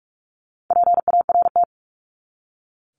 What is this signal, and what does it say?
Morse code, heard as a keyed tone, about 44 wpm, 700 Hz. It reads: PART
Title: ES:S Suffix - Top 100 Words with Base Words 45wpm